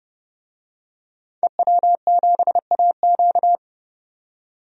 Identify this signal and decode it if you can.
Morse code, heard as a keyed tone, about 29 wpm, 700 Hz. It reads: EW7AQ
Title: DXCC-100 Call Signs - 5 Characters - 30wpm